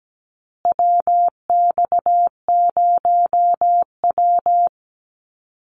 Morse code, heard as a keyed tone, about 17 wpm, 700 Hz. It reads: WX0W
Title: USA Call Signs Repeated 2x 17wpm